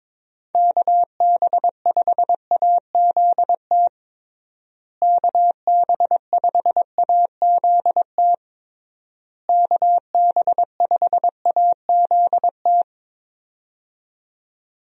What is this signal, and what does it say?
Morse code, heard as a keyed tone, about 22 wpm, 700 Hz. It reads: KB5AZT KB5AZT KB5AZT
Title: USA Call Signs - 6 Characters - Repeated 3x - 22wpm